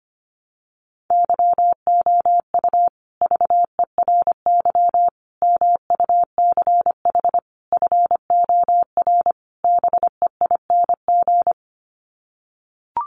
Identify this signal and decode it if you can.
Morse code, heard as a keyed tone, about 25 wpm, 700 Hz. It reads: YOU VERY MUCH FOR BEING
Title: Sets of 5 Words Encore 25wpm